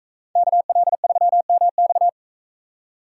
Morse code, heard as a keyed tone, about 44 wpm, 700 Hz. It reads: KL3MX